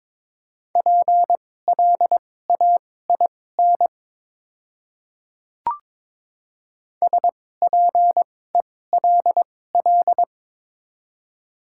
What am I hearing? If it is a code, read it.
Morse code, heard as a keyed tone, about 22 wpm, spaced as for 16 wpm, 700 Hz. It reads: PLAIN SPELL
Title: 2x Character Spacing - Top 500 Words - 22wpm